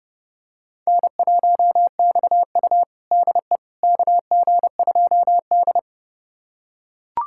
Morse code, heard as a keyed tone, about 32 wpm, 700 Hz. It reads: N1XU DE KG2D
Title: USA Callsign DE Callsign - 4 Characters - 30wpm